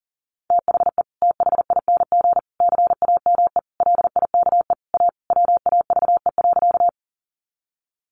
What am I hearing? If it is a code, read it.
Morse code, heard as a keyed tone, about 40 wpm, 700 Hz. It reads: THE THING CAME LIKE A WAVE.
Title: Sentences from Top 1400 Words 40wpm